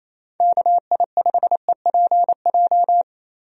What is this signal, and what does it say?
Morse code, heard as a keyed tone, about 28 wpm, 700 Hz. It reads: KI5EPJ
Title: USA Call Signs - 6 Characters - Repeated 2x - 28wpm